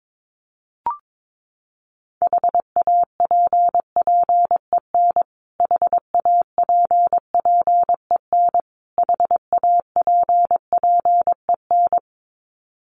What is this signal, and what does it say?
Morse code, heard as a keyed tone, about 22 wpm, 700 Hz. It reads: HAPPEN HAPPEN HAPPEN
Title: Top 500 Words Repeated 22wpm